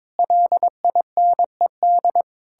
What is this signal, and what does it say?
Morse code, heard as a keyed tone, about 22 wpm, 700 Hz. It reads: LINED
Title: Pangram Challenge v2 - 22wpm